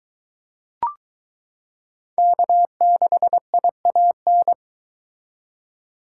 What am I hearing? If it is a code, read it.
Morse code, heard as a keyed tone, about 21 wpm, 700 Hz. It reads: K6IAN